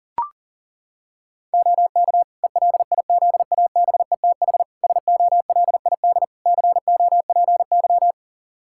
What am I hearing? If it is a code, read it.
Morse code, heard as a keyed tone, about 37 wpm, 700 Hz. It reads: OK ELIZABETH SOLID COPY
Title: Practice QSO - Exchange 3 - More Info - 40wpm